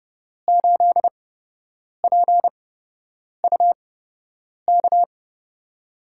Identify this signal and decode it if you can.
Morse code, heard as a keyed tone, about 30 wpm, spaced as for 8 wpm, 700 Hz. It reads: 8PUK
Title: USA Call Signs - 8x Character Spacing - 30wpm